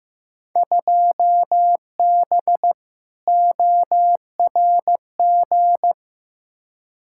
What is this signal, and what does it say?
Morse code, heard as a keyed tone, about 16 wpm, 700 Hz. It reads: 2B ORG
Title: Field Day Exchanges Encore - 15wpm